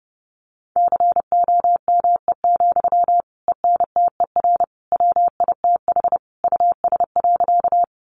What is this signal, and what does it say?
Morse code, heard as a keyed tone, about 30 wpm, 700 Hz. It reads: COME, ENTER WITH US.
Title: Sentences from Top 1300 Words 30wpm